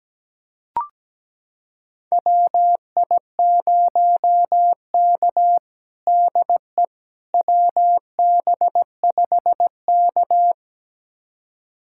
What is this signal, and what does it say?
Morse code, heard as a keyed tone, about 17 wpm, 700 Hz. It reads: WI0K DE WB5K
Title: USA Callsign DE Callsign - 4 Characters - 17wpm